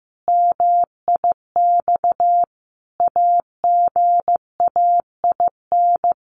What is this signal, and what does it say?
Morse code, heard as a keyed tone, about 15 wpm, 700 Hz. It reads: MIX AGAIN
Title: Sentences from Top 1100 Words 15wpm